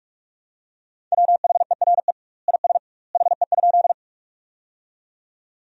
Morse code, heard as a keyed tone, about 42 wpm, 700 Hz. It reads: WHERE IS HE?
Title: 2x Word-Spacing - Sentences from Top 200 Words - 45wpm